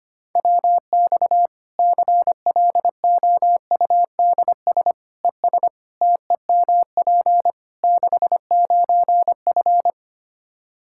Morse code, heard as a keyed tone, about 25 wpm, 700 Hz. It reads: WX CLOUDS ES TEMP 69F
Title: K1USN SST QSO Practice 25wpm